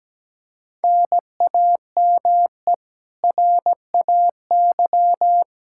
Morse code, heard as a keyed tone, about 17 wpm, 700 Hz. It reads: NAME RAY
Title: Practice QSO - Exchange 1 - Beginning - 17wpm